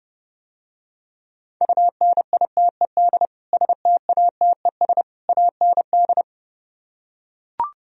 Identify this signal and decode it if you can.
Morse code, heard as a keyed tone, about 32 wpm, 700 Hz. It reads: UNITED STATES AND